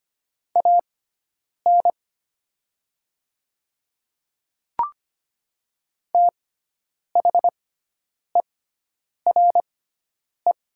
Morse code, heard as a keyed tone, about 25 wpm, spaced as for 9 wpm, 700 Hz. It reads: AN THERE
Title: 6x Character Spacing - Top 300 Words - 25wpm